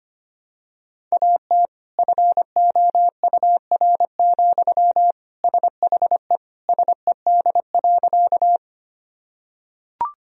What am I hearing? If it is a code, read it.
Morse code, heard as a keyed tone, about 25 wpm, 700 Hz. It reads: AT FOUR, SHE SED.
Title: Sentences from Top 300 Words and Abbreviations 25wpm